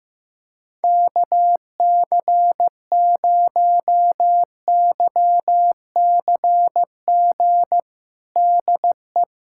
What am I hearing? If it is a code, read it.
Morse code, heard as a keyed tone, about 15 wpm, 700 Hz. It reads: KC0YCG DE
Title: USA Callsign DE Callsign - 6 Characters - 15wpm